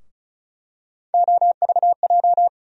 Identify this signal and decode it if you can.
Morse code, heard as a keyed tone, about 34 wpm, 700 Hz. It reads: OVJ
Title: USA Call Sign Postfixes - 3 Letters 35wpm